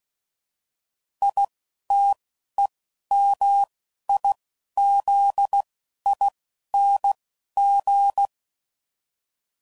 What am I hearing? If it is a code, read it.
Morse code, heard as a keyed tone, about 16 wpm, 800 Hz. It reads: I T E M I Z I N G
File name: Suffixes / Special INGs / Single INGs